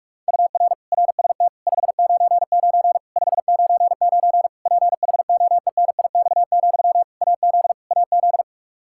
Morse code, heard as a keyed tone, about 46 wpm, 700 Hz. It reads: UR RST 599 599 PHOENIX, AZ AZ
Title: K1USN SST QSO Practice 45wpm